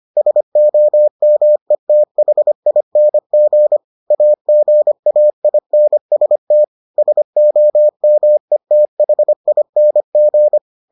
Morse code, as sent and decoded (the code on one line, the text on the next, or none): ... --- -- . - .... .. -. --. / .- --. .- .. -. ... - / ... --- -- . - .... .. -. --.
SOMETHING AGAINST SOMETHING